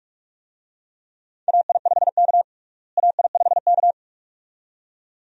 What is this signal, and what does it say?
Morse code, heard as a keyed tone, about 46 wpm, 700 Hz. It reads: AI5K AI5K
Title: USA Call Signs - 4 Characters - Repeated 2x - 45wpm